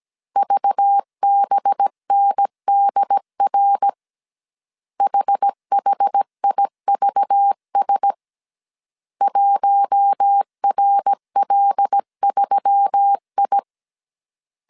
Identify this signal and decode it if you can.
Morse code, heard as a keyed tone, about 17 wpm, 800 Hz. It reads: VBNDR HHIVS 1RL3I
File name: CW / DET / 1 6 1500C S A 16wpm 2635